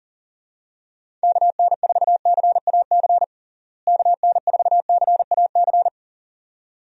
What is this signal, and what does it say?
Morse code, heard as a keyed tone, about 40 wpm, 700 Hz. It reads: KN4CAC KN4CAC
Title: USA Call Signs - 6 Characters - Repeated 2x - 40wpm